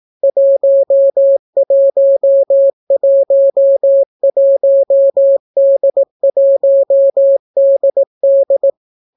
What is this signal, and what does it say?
Morse code, heard as a keyed tone, about 18 wpm, 550 Hz. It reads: 1111D1DD